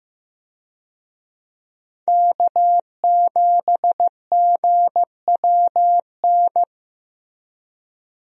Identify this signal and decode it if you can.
Morse code, heard as a keyed tone, about 16 wpm, 700 Hz. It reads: K7GWN
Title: USA Call Signs - 5 Characters - Repeated 3x - 15wpm